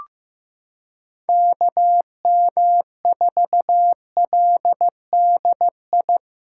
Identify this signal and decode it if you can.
Morse code, heard as a keyed tone, about 16 wpm, 700 Hz. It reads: KM4LDI
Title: USA Callsign DE Callsign - 6 Characters - 15wpm